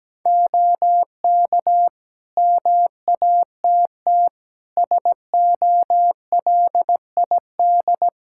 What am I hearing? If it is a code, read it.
Morse code, heard as a keyed tone, about 17 wpm, 700 Hz. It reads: OK MATT SOLID
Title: Practice QSO - Exchange 3 - More Info - 17wpm